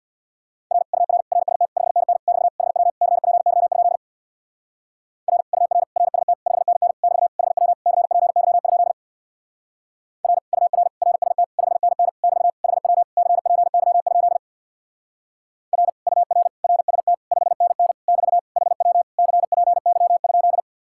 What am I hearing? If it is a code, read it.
Morse code, heard as a keyed tone, about 66 wpm, 700 Hz. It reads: R UR RST 5NN <BT> HW CPY? R UR RST 5NN <BT> HW CPY? R UR RST 5NN <BT> HW CPY? R UR RST 5NN <BT> HW CPY?